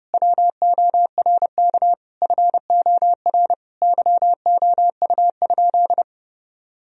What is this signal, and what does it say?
Morse code, heard as a keyed tone, about 30 wpm, 700 Hz. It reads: WORK FOR YOU?
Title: Sentences from Top 1000 Words - Review Part 4 - 30wpm